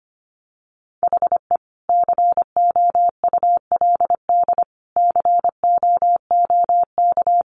Morse code, heard as a keyed tone, about 25 wpm, 700 Hz. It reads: HE COULD COOK
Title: Sentences from Top 700 Words 25wpm